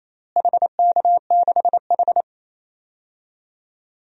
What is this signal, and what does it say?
Morse code, heard as a keyed tone, about 28 wpm, 700 Hz. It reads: HK6H
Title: DXCC-100 Call Signs - 4 Characters - Repeated 2x - 28wpm